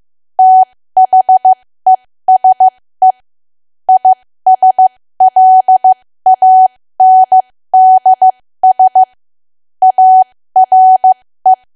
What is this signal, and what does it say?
Morse code, heard as a keyed tone, about 15 wpm, spaced as for 13 wpm, 750 Hz. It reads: THESE ISLANDS ARE
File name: cwa / W1AW-Level2 / 13WPM